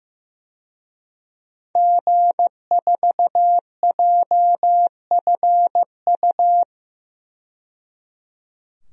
Morse code, heard as a keyed tone, about 15 wpm, 700 Hz. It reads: G4JFU
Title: DXCC-100 Call Signs - 5 Characters - 15wpm